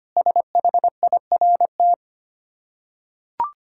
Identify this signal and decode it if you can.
Morse code, heard as a keyed tone, about 25 wpm, 700 Hz. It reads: SHIRT
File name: practice / 25wpm / Difficult Words with Hints - 25wpm